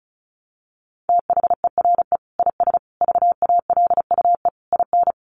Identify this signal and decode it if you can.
Morse code, heard as a keyed tone, about 34 wpm, 700 Hz. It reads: THERE IS VALUE IN
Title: Sentences from Top 1100 Words 35wpm